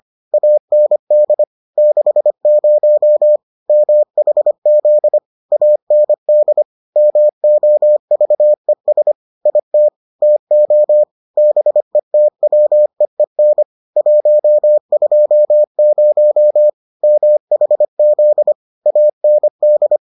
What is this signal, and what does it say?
Morse code, heard as a keyed tone, about 25 wpm, 600 Hz. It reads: AND 60 MHZ AND MOVES IT TO BETWEEN 120 MHZ AND